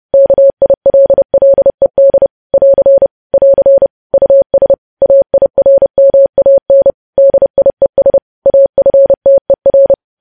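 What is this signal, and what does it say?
Morse code, heard as a keyed tone, about 30 wpm, 550 Hz. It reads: KILLED <AR> <AR> US AIRMAN DIES AFTER
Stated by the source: News headlines sent in Morse code at 30.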